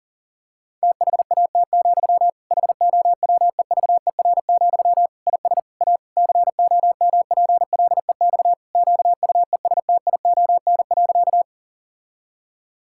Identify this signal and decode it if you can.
Morse code, heard as a keyed tone, about 40 wpm, 700 Hz. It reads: THAT, HOWEVER, IS A COMPLEX QUESTION.
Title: Sentences from Top 1000 Words - Review Part 6 - 40wpm